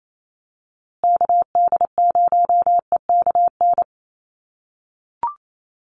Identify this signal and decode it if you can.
Morse code, heard as a keyed tone, about 28 wpm, 700 Hz. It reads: KD0EKN
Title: USA Call Signs Repeated 3x 28wpm